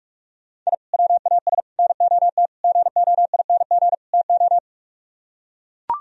Morse code, heard as a keyed tone, about 44 wpm, 700 Hz. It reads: I WAS NOT GOING TO